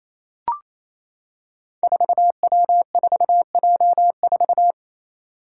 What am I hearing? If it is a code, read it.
Morse code, heard as a keyed tone, about 28 wpm, 700 Hz. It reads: 4W4J4
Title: Five Challenging Letter-Number Combinations 28wpm